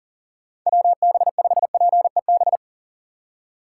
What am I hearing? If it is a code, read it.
Morse code, heard as a keyed tone, about 40 wpm, 700 Hz. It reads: WB5PEB
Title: USA Call Signs - 6 Characters - 2x Character Spacing - 40wpm